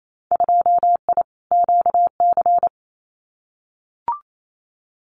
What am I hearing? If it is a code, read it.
Morse code, heard as a keyed tone, about 28 wpm, 700 Hz. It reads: 2I QC
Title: Winter Field Day Exchanges 28wpm